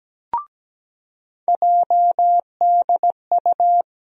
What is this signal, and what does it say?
Morse code, heard as a keyed tone, about 17 wpm, 700 Hz. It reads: JDU
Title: USA Call Sign Postfixes - 3 Letters 17wpm